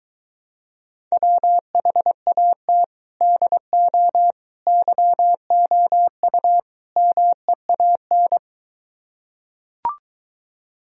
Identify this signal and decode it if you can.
Morse code, heard as a keyed tone, about 23 wpm, 700 Hz. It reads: WHAT DO YOU MEAN